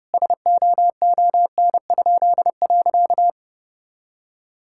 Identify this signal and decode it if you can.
Morse code, heard as a keyed tone, about 32 wpm, 700 Hz. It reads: SOON?.
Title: Pangram Challenge v2 - 30wpm